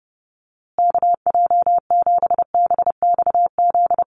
Morse code, heard as a keyed tone, about 30 wpm, 700 Hz. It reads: KJ7BXZ
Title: USA Call Signs Repeated 3x 30wpm